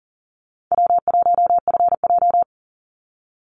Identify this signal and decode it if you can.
Morse code, heard as a keyed tone, about 40 wpm, 700 Hz. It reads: W1FJ
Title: Easy Call Signs - 40wpm